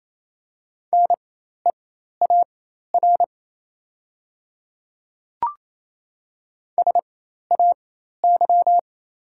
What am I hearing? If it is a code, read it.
Morse code, heard as a keyed tone, about 27 wpm, spaced as for 13 wpm, 690 Hz. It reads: NEAR SAY
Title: 4x Character Spacing - Top 200 Words - 28wpm